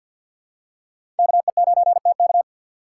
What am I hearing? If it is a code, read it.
Morse code, heard as a keyed tone, about 52 wpm, 700 Hz. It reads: KE9TK
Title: USA Call Signs - 5 Characters - 3x Character Spacing - 50wpm